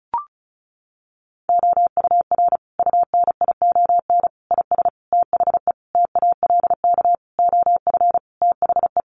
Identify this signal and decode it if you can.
Morse code, heard as a keyed tone, about 34 wpm, 700 Hz. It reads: OUR UNION IS THE TALK OF THE
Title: Sentences from Top 1400 Words 35wpm